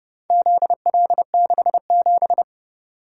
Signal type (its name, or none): Morse code